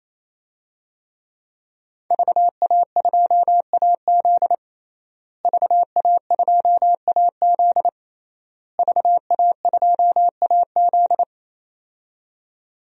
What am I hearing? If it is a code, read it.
Morse code, heard as a keyed tone, about 28 wpm, 700 Hz. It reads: VA2AZ VA2AZ VA2AZ
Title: DXCC-100 Call Signs - 5 Characters - Repeated 3x - 28wpm